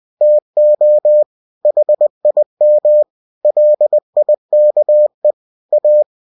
Morse code, heard as a keyed tone, about 20 wpm, 600 Hz. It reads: TO HIM LIKE A
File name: cw-books / winnie-the-pooh / winnie-20wpm-page-0045